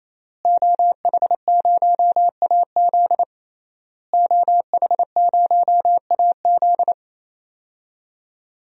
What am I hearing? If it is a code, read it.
Morse code, heard as a keyed tone, about 28 wpm, 700 Hz. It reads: OH0AZ OH0AZ
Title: DXCC-100 Call Signs - 5 Characters - Repeated 3x - 28wpm